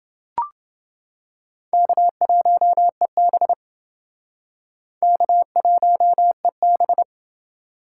Morse code, heard as a keyed tone, about 29 wpm, 700 Hz. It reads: K1EB K1EB